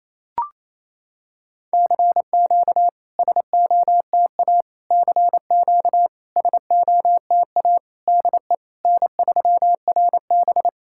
Morse code, heard as a keyed tone, about 28 wpm, 690 Hz. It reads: CQ SOTA CQ SOTA DE N3RB
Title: SOTA QSO Practice 28wpm